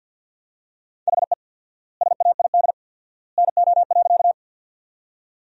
Morse code, heard as a keyed tone, about 50 wpm, 700 Hz. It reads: HE SAID NO.
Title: 4x Word-Spacing - Sentences from Top 100 Words - 50wpm